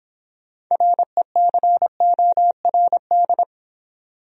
Morse code, heard as a keyed tone, about 24 wpm, 700 Hz. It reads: RECORD